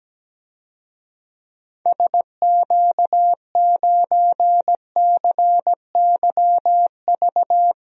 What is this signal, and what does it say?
Morse code, heard as a keyed tone, about 17 wpm, 700 Hz. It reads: SQ9CYV
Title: DXCC-100 Call Signs - 6 Characters - Repeated 3x - 17wpm